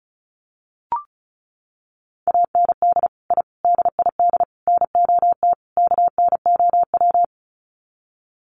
Morse code, heard as a keyed tone, about 34 wpm, 700 Hz. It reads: AND I DID NOT KNOW
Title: Sets of 5 Words Encore 35wpm